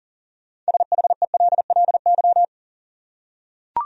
Morse code, heard as a keyed tone, about 40 wpm, 700 Hz. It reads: SHELLY
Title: USA Common Names - 40wpm